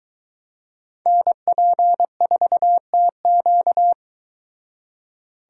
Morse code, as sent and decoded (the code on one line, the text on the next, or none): -. .--. ....- - --.-
NP4TQ